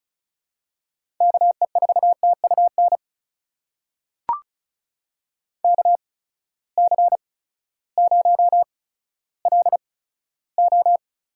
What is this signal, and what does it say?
Morse code, heard as a keyed tone, about 36 wpm, spaced as for 16 wpm, 700 Hz. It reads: KE4TUN KC0LO
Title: USA Call Signs - 6 Characters - 8x Character Spacing - 35wpm